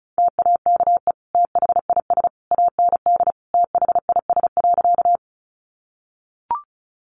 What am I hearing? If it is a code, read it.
Morse code, heard as a keyed tone, about 34 wpm, 700 Hz. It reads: TAKE THIS AND THIS.
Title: Sentences from Top 200 Words 35wpm